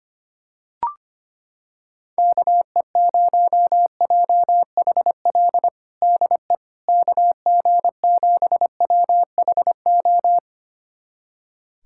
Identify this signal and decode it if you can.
Morse code, heard as a keyed tone, about 25 wpm, 700 Hz. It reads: KE0JHL DE KG7WHO
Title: USA Callsign DE Callsign 25wpm